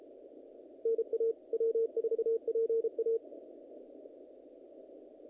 Morse code, heard as a keyed tone, about 33 wpm, 450 Hz. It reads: NA W4PA
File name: contests / naqp-cw-k5tr-op-k5pi- -2100